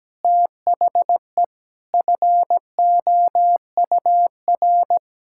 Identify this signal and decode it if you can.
Morse code, heard as a keyed tone, about 17 wpm, 700 Hz. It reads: THE FOUR
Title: Pangram Challenge v2 - 17wpm